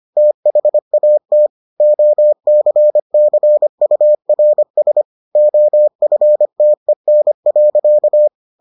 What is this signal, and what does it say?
Morse code, heard as a keyed tone, about 25 wpm, 600 Hz. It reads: THAT OCCURS OFTEN.